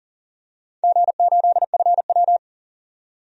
Morse code, heard as a keyed tone, about 38 wpm, 700 Hz. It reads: G8FW
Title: DXCC-100 Call Signs - 4 Characters - 40wpm